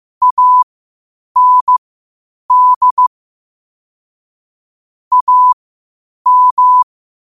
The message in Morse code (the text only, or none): AND AM